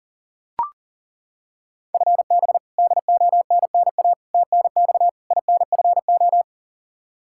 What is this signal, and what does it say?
Morse code, heard as a keyed tone, about 40 wpm, 700 Hz. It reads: FB DONNA TNX INFO
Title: Practice QSO - Exchange 3 - More Info - 40wpm